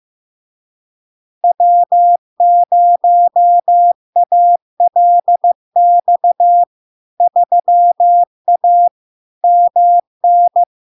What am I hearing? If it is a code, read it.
Morse code, heard as a keyed tone, about 15 wpm, 700 Hz. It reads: W0ALX 3A MN
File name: LARK-FD-2022-Log-in-CW-15wpm-700-Hz